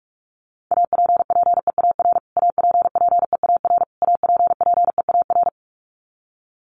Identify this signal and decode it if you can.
Morse code, heard as a keyed tone, about 44 wpm, 700 Hz. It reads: APPEAR APPEAR APPEAR
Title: Top 500 Words Repeated 45wpm